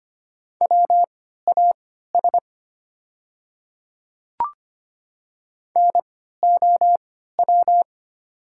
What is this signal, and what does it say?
Morse code, heard as a keyed tone, about 25 wpm, spaced as for 14 wpm, 700 Hz. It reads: WAS NOW
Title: 3x Character Spacing - Top 100 Words - 25wpm